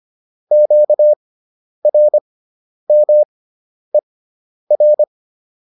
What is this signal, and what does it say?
Morse code, heard as a keyed tone, about 25 wpm, spaced as for 10 wpm, 600 Hz. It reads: QRMER